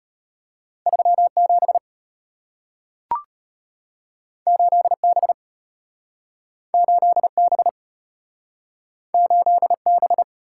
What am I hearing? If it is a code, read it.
Morse code, heard as a keyed tone, about 33 wpm, 700 Hz. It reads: 3 7 8 B 8 B 8 B